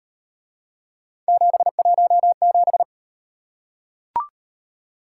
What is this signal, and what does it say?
Morse code, heard as a keyed tone, about 37 wpm, 700 Hz. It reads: 717